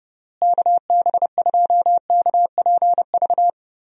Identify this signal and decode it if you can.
Morse code, heard as a keyed tone, about 30 wpm, 700 Hz. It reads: KB2KPV
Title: USA Call Signs - 6 Characters - Repeated 3x - 30wpm